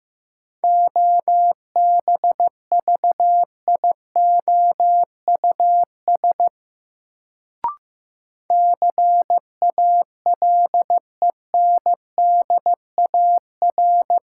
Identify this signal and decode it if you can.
Morse code, heard as a keyed tone, about 15 wpm, 700 Hz. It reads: OBVIOUS CALENDAR
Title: Sentences from Top 1600 Words - New Word Drill Alt - 15wpm